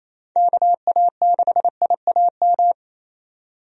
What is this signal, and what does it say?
Morse code, heard as a keyed tone, about 28 wpm, 700 Hz. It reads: KA6IAM
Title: USA Call Signs - 6 Characters - 28wpm